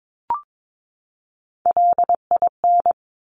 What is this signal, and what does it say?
Morse code, heard as a keyed tone, about 22 wpm, 700 Hz. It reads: LIN
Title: Top Three Letter Combinations in Words 22wpm